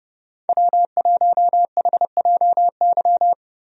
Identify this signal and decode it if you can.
Morse code, heard as a keyed tone, about 30 wpm, 700 Hz. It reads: W1HJY
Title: USA Call Signs - 5 Characters - Repeated 2x - 30wpm